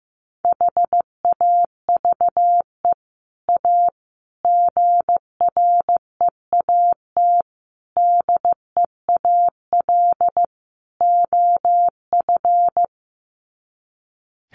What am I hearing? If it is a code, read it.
Morse code, heard as a keyed tone, about 15 wpm, 700 Hz. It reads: HAVE A GREAT DEAL OF
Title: Sets of 5 Words Encore2 15wpm